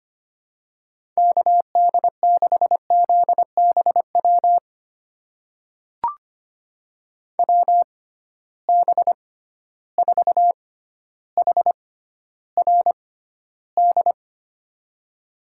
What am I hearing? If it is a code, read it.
Morse code, heard as a keyed tone, about 25 wpm, spaced as for 13 wpm, 700 Hz. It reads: KD6ZBW WB4HRD
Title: USA Call Signs - 6 Characters - 6x Character Spacing - 25wpm